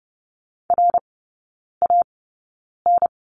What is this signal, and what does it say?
Morse code, heard as a keyed tone, about 30 wpm, spaced as for 9 wpm, 700 Hz. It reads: RAN
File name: Lesson 16e - Letter c - Three Characters 30wpm